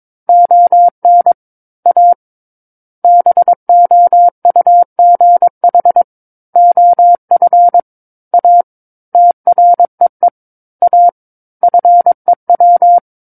Morse code, heard as a keyed tone, about 22 wpm, 700 Hz. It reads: ON A BOUGH OF A TREE A FEW
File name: 06-Chapter-06 22 WPM